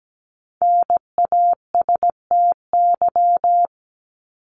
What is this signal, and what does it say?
Morse code, heard as a keyed tone, about 17 wpm, 700 Hz. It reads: NASTY
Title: Top 1751-2000 Words 17wpm